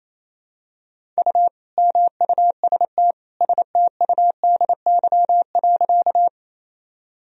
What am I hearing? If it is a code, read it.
Morse code, heard as a keyed tone, about 28 wpm, 700 Hz. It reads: U MUST STUDY.
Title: Sentences from Top 300 Words and Abbreviations 28wpm